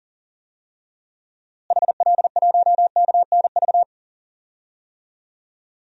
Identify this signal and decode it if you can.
Morse code, heard as a keyed tone, about 40 wpm, 700 Hz. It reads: HL1KNV
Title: DXCC-100 Call Signs - 6 Characters - Repeated 3x - 40wpm